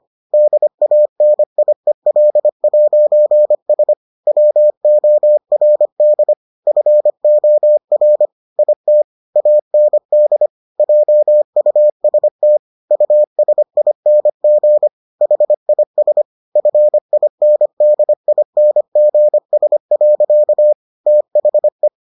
Morse code, heard as a keyed tone, about 25 wpm, 600 Hz. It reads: DANIEL'S WORD FOR IT AND JUST USING HIS FINDINGS. THE